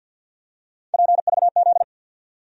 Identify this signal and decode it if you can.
Morse code, heard as a keyed tone, about 49 wpm, 700 Hz. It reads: PVZ
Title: USA Call Sign Postfixes - 3 Letters 50wpm